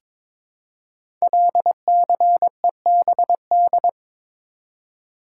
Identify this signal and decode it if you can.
Morse code, heard as a keyed tone, about 22 wpm, 700 Hz. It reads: LCEBD
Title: Code Groups - 5 Letters - 22wpm